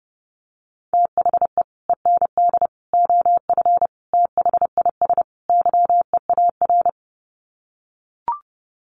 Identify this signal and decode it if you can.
Morse code, heard as a keyed tone, about 30 wpm, 700 Hz. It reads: THE END OF THIS YEAR
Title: Sets of 5 Words Encore 30wpm